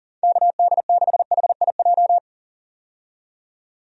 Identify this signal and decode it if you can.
Morse code, heard as a keyed tone, about 40 wpm, 700 Hz. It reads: KD6HIJ
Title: USA Call Signs - 6 Characters - 40wpm